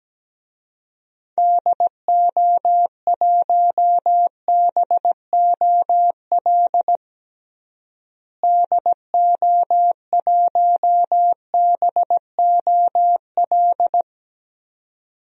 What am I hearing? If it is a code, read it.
Morse code, heard as a keyed tone, about 17 wpm, 700 Hz. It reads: DO1BOL DO1BOL
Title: DXCC-100 Call Signs - 6 Characters - Repeated 2x - 17wpm